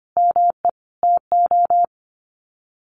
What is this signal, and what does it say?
Morse code, heard as a keyed tone, about 25 wpm, 700 Hz. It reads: ME TO
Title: Sets of 5 Words 25wpm